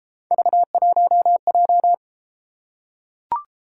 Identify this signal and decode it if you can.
Morse code, heard as a keyed tone, about 31 wpm, 700 Hz. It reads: V1J